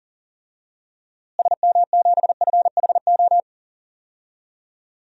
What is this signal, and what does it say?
Morse code, heard as a keyed tone, about 40 wpm, 700 Hz. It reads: SM7FHO
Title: DXCC-100 Call Signs - 6 Characters - Repeated 3x - 40wpm